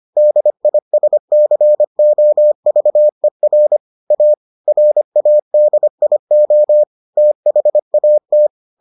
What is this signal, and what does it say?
Morse code, heard as a keyed tone, about 25 wpm, 600 Hz. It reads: DISCOVER A RADIO THAT